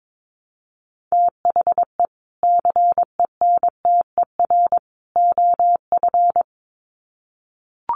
Morse code, heard as a keyed tone, about 22 wpm, 700 Hz. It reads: THE CENTER OF
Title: Sets of 3 Words 22wpm